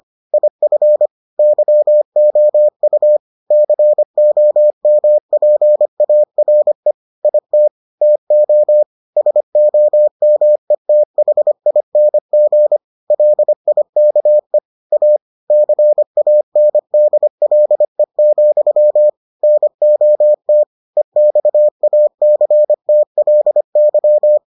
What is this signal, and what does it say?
Morse code, heard as a keyed tone, about 25 wpm, 600 Hz. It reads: IF YOU COMPARE IT TO SOMETHING LIKE A CANDLE, NOT EXACTLY